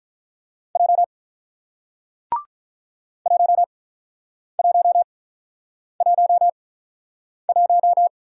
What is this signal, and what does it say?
Morse code, heard as a keyed tone, about 48 wpm, spaced as for 9 wpm, 700 Hz. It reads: J 1111